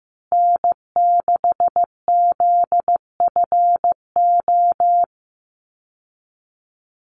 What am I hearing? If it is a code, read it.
Morse code, heard as a keyed tone, about 15 wpm, 700 Hz. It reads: N6ZFO
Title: Call Signs Repeated 3x - 15wpm